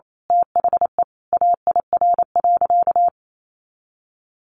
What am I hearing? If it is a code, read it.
Morse code, heard as a keyed tone, about 28 wpm, 700 Hz. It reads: THE AIR.
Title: A Sound Of Thunder 28wpm